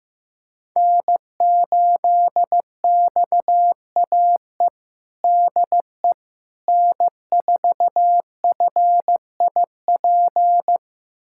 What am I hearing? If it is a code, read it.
Morse code, heard as a keyed tone, about 15 wpm, 700 Hz. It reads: N8XAE DE N4FIP
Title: USA Callsign DE Callsign - 5 Characters - 15wpm